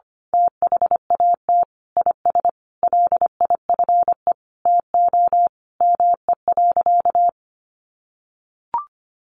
Morse code, heard as a keyed tone, about 25 wpm, 700 Hz. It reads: THAT IS LIFE TO ME.
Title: Sentences from Top 500 Words 25wpm